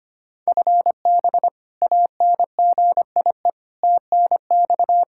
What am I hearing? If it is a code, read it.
Morse code, heard as a keyed tone, about 25 wpm, 700 Hz. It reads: FB ANGIE TNX
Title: K1USN SST QSO Practice 25wpm